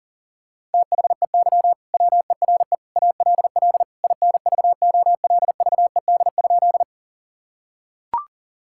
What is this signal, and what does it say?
Morse code, heard as a keyed tone, about 40 wpm, 700 Hz. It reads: THEY WERE ALL INVOLVED?
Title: Sentences from Top 1000 Words - Review Part 5 - 40wpm